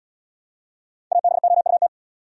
Morse code, heard as a keyed tone, about 78 wpm, 700 Hz. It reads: ABOUT